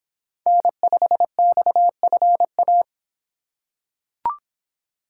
Morse code, heard as a keyed tone, about 24 wpm, 700 Hz. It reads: N5XFA